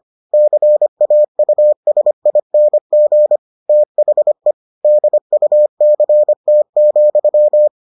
Morse code, heard as a keyed tone, about 25 wpm, 600 Hz. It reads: CAUSING THE DUCT,